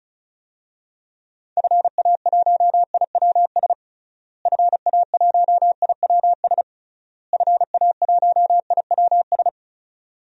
Morse code, heard as a keyed tone, about 34 wpm, 700 Hz. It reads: FA1IWS FA1IWS FA1IWS
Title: DXCC-100 Call Signs - 6 Characters - Repeated 3x - 35wpm